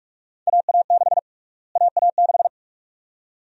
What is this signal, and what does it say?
Morse code, heard as a keyed tone, about 44 wpm, 700 Hz. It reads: AA6 AA6
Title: USA Call Sign Prefixes - Repeated 2x - 45wpm